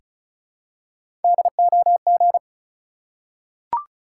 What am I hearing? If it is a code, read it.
Morse code, heard as a keyed tone, about 36 wpm, 700 Hz. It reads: DOG
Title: Three Letter Words - 35wpm